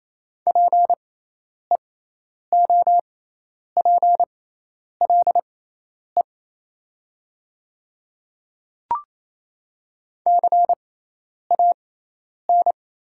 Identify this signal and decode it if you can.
Morse code, heard as a keyed tone, about 27 wpm, spaced as for 10 wpm, 700 Hz. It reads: PEOPLE CAN
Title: 6x Character Spacing - Top 100 Words - 28wpm